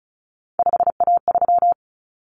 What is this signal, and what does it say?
Morse code, heard as a keyed tone, about 34 wpm, 700 Hz. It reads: 5A3
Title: Common Cut Numbers - 3-Digit 35wpm